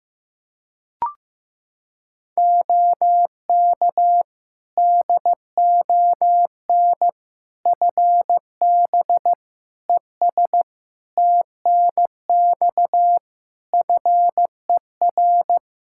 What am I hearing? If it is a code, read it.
Morse code, heard as a keyed tone, about 15 wpm, 700 Hz. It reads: OK DON FB ES TNX FER
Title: Practice QSO - Exchange 2 - Info - 15wpm